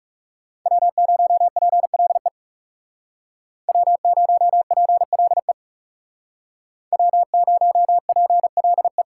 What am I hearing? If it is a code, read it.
Morse code, heard as a keyed tone, about 37 wpm, 700 Hz. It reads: W0PLE W0PLE W0PLE
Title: Speed Racing - USA 1x3 Call Signs 30wpm